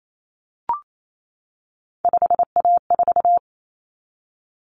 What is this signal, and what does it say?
Morse code, heard as a keyed tone, about 28 wpm, 700 Hz. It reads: 5A4
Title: Common Cut Numbers - 3-Digit 28wpm